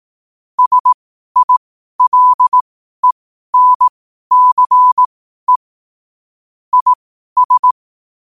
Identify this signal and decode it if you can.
Morse code, heard as a keyed tone, about 18 wpm, spaced as for 12 wpm, 1000 Hz. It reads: SILENCE IS
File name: qotdfiles / 12wpm